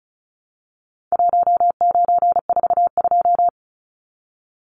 Morse code, heard as a keyed tone, about 34 wpm, 700 Hz. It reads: 1942
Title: Numbers - Contest Style - 4-Digit 35wpm